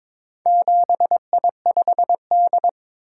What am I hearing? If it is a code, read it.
Morse code, heard as a keyed tone, about 22 wpm, 700 Hz. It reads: 7I5D
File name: practice / 22wpm / Four Letter-Number Combinations - 22wpm